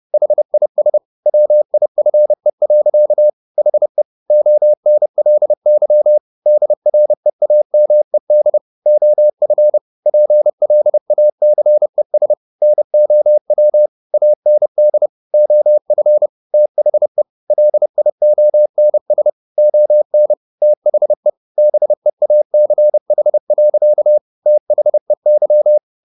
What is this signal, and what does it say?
Morse code, heard as a keyed tone, about 30 wpm, 600 Hz. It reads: HIS WIFE. HE ONLY DREAMED OF PLACES NOW AND OF THE LIONS ON THE BEACH. THEY
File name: the old man and the sea / the old man and the sea 1 30wpm